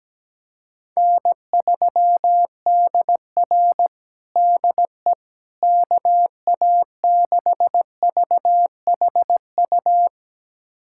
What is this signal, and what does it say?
Morse code, heard as a keyed tone, about 17 wpm, 700 Hz. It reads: N3DR DE KA6VHU
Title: USA Callsign DE Callsign 17wpm